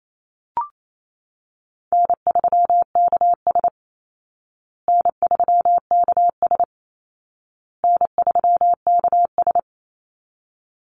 Morse code, heard as a keyed tone, about 28 wpm, 700 Hz. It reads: N3KS N3KS N3KS
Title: Easy Call Signs Repeated 3x - 28wpm